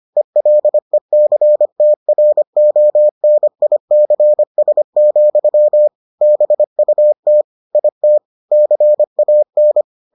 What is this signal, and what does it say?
Morse code, heard as a keyed tone, about 25 wpm, 600 Hz. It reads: ELECTRONICS, BUT IT CAN